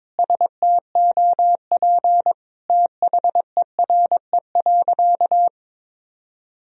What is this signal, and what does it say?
Morse code, heard as a keyed tone, about 22 wpm, 700 Hz. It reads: STOP THERE.
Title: Sentences from Top 1000 Words - Review Part 1 - 22wpm